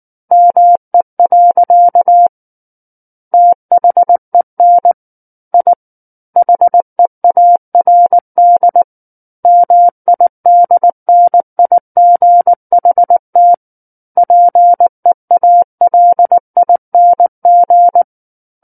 Morse code, heard as a keyed tone, about 19 wpm, 700 Hz. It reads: ME. THEN I HEARD MIDNIGHT PEALING
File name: Book One / 10-Chapter 10 - 19 WPM